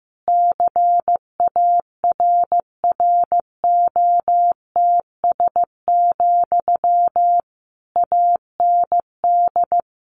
Morse code, heard as a keyed tone, about 15 wpm, 700 Hz. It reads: CARROTS, AND
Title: Easy Phrases 15wpm